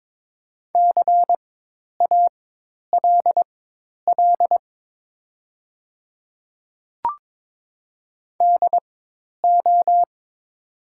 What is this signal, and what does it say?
Morse code, heard as a keyed tone, about 22 wpm, spaced as for 10 wpm, 700 Hz. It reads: CALL DO
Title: 4x Character Spacing - Top 100 Words - 22wpm